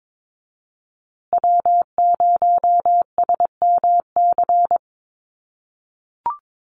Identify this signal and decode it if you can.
Morse code, heard as a keyed tone, about 22 wpm, 700 Hz. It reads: W0SMC
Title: USA Call Signs Repeated 2x 22wpm